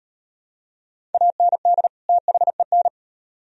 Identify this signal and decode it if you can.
Morse code, heard as a keyed tone, about 38 wpm, 700 Hz. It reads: AND THEN